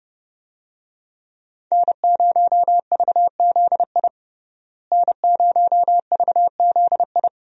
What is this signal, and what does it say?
Morse code, heard as a keyed tone, about 32 wpm, 700 Hz. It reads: N0VZI N0VZI
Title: USA Call Signs - 5 Characters - Repeated 2x - 30wpm